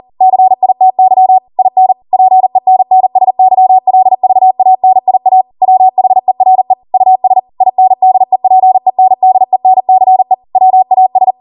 Morse code, heard as a keyed tone, about 40 wpm, 750 Hz. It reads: CITY IN PENNSYLVANIA WHERE US INDEPENDENCE WAS